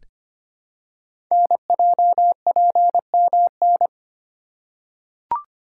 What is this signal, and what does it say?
Morse code, heard as a keyed tone, about 25 wpm, 700 Hz. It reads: NJPMN
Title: Code Groups - 5 Letters - 25wpm